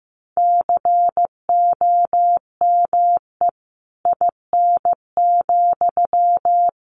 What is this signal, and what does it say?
Morse code, heard as a keyed tone, about 15 wpm, 700 Hz. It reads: COME IN,
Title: Sentences from Top 1500 Words 15wpm